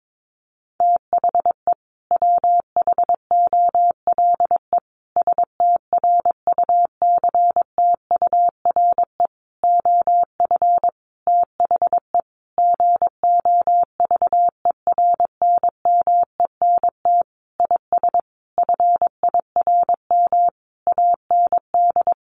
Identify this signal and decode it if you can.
Morse code, heard as a keyed tone, about 22 wpm, 700 Hz. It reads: THE WHOLE STRUCTURE OF THE GOVERNMENT IS FIRM AND
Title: Sentences from Top 1500 Words 22wpm